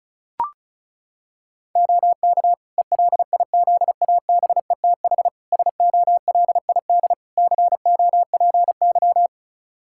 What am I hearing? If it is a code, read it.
Morse code, heard as a keyed tone, about 34 wpm, 700 Hz. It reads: OK ELIZABETH SOLID COPY
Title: Practice QSO - Exchange 3 - More Info - 35wpm